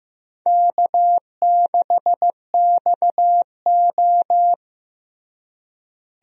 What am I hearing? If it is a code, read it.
Morse code, heard as a keyed tone, about 15 wpm, 700 Hz. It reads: K6XO
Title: USA Call Signs - 4 Characters - Repeated 3x - 15wpm